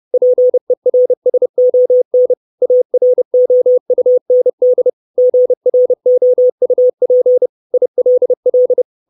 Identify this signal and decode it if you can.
Morse code, heard as a keyed tone, about 30 wpm, 500 Hz. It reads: PERSON AROUND GROUP ILL
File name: 632 Words 30WPM